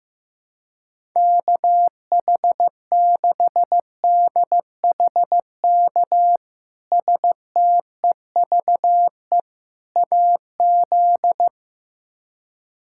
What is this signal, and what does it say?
Morse code, heard as a keyed tone, about 15 wpm, 700 Hz. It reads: KH6DHK STEVE AZ
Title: K1USN SST QSO Practice - 15wpm